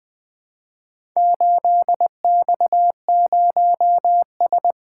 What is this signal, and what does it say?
Morse code, heard as a keyed tone, about 20 wpm, 700 Hz. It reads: 8X0S
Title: Four Letter-Number Combinations - 20wpm